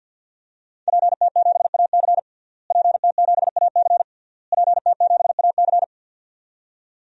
Morse code, heard as a keyed tone, about 51 wpm, 700 Hz. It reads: PT7AC PT7AC PT7AC
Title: DXCC-100 Call Signs - 5 Characters - Repeated 3x - 50wpm